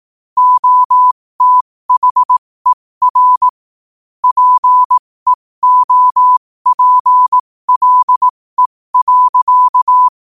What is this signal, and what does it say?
Morse code, heard as a keyed tone, about 18 wpm, spaced as for 15 wpm, 1000 Hz. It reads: OTHER PEOPLE.
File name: qotdfiles / 15wpm